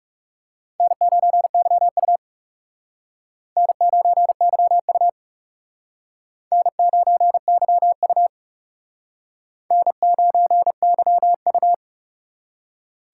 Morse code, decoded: N9YU N9YU N9YU N9YU

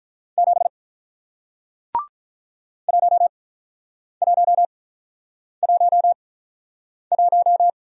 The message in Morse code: 7 1111

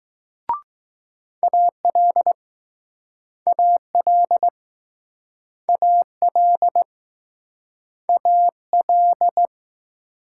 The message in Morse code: AL AL AL AL

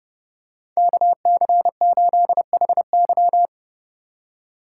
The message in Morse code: KC8HY

